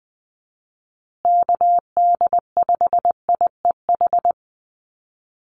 Morse code, heard as a keyed tone, about 20 wpm, 700 Hz. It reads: KD5IEH
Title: USA Call Signs Repeated 2x 20wpm